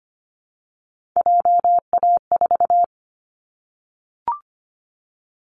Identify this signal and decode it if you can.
Morse code, heard as a keyed tone, about 25 wpm, 700 Hz. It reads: JA4
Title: Callsign Prefixes for Top 25 DXCC - 25wpm